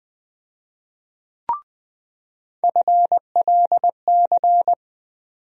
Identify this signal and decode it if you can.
Morse code, heard as a keyed tone, about 20 wpm, 700 Hz. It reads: FLC
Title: USA Call Sign Postfixes - 3 Letters 20wpm